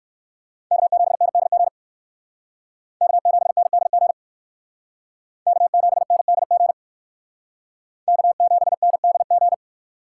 Morse code, heard as a keyed tone, about 60 wpm, 700 Hz. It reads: K7NDG K7NDG K7NDG K7NDG